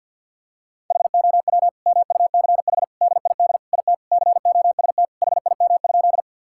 Morse code, heard as a keyed tone, about 46 wpm, 700 Hz. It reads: HOW MUCH DID IT COST HIM?
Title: Sentences from Top 1000 Words - Review Part 5 - 50wpm